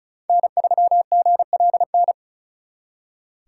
Morse code, heard as a keyed tone, about 45 wpm, 700 Hz. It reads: N3GLN